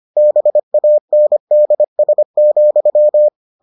Morse code, heard as a keyed tone, about 25 wpm, 600 Hz. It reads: BANDS,